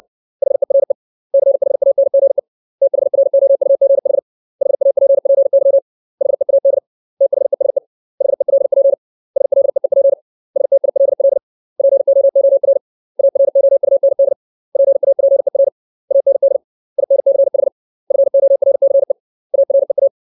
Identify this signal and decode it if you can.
Morse code, heard as a keyed tone, about 60 wpm, 550 Hz. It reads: HERE CHANGE THROUGH HAPPY HEAD THESE HELP SLEEP STILL POOR AROUND PAPER AND INCH FORCE AREA
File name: 632 words / 632 Words 60WPM